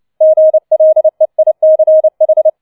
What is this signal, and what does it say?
Morse code, heard as a keyed tone, about 29 wpm, 630 Hz. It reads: GLEICH
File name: 006 Sechster Streich 30wpm